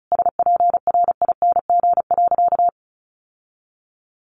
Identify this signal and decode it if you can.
Morse code, heard as a keyed tone, about 34 wpm, 700 Hz. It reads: SPRING.
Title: Sentences from Top 1200 Words 35wpm